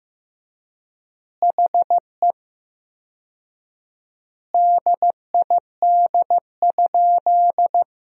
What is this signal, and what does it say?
Morse code, heard as a keyed tone, about 15 wpm, 700 Hz. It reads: HE DID?
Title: 4x Word-Spacing - Sentences from Top 100 Words - 15wpm